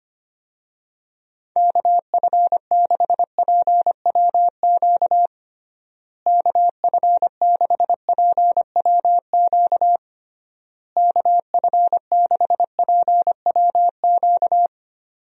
Morse code, heard as a keyed tone, about 25 wpm, 700 Hz. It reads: KF6PWQ KF6PWQ KF6PWQ
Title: USA Call Signs - 6 Characters - Repeated 3x - 25wpm